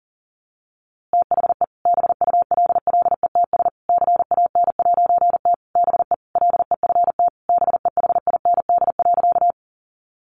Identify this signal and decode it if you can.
Morse code, heard as a keyed tone, about 40 wpm, 700 Hz. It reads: THE BULLETS CAN'T BE LEFT BEHIND.
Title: A Sound Of Thunder 40wpm